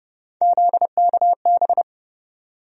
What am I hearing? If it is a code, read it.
Morse code, heard as a keyed tone, about 30 wpm, 700 Hz. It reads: ZKB
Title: USA Call Sign Postfixes - 3 Letters 30wpm